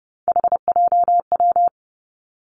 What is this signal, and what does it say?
Morse code, heard as a keyed tone, about 30 wpm, 700 Hz. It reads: HJW
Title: Three Challenging Letter-Number Combinations 30wpm